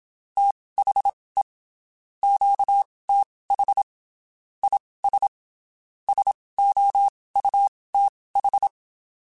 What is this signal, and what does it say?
Morse code, heard as a keyed tone, about 27 wpm, spaced as for 19 wpm, 800 Hz. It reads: THE QTH IS SOUTH
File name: (spaced) / QSOs Spaced-mp3s / kc7sac @25 sp